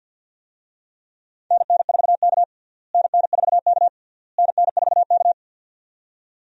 Morse code, heard as a keyed tone, about 47 wpm, 700 Hz. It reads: NN4K NN4K NN4K
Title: USA Call Signs - 4 Characters - Repeated 3x - 50wpm